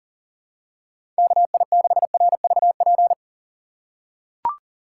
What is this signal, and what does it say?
Morse code, heard as a keyed tone, about 40 wpm, 700 Hz. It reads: KI6RVP
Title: USA Call Signs - 6 Characters - 3x Character Spacing - 40wpm